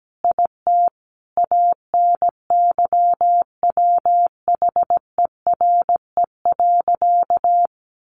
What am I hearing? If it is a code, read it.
Morse code, heard as a keyed tone, about 17 wpm, 700 Hz. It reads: IT ANYWHERE.
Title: Sentences from Top 1400 Words 17wpm